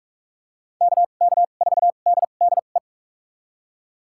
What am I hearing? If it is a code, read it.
Morse code, heard as a keyed tone, about 46 wpm, spaced as for 33 wpm, 700 Hz. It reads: KK4DDE
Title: USA Call Signs - 6 Characters - 2x Character Spacing - 45wpm